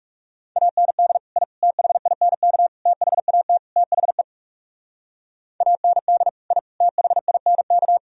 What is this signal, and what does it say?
Morse code, heard as a keyed tone, about 43 wpm, 700 Hz. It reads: AND I THINK THAT THE AND I THINK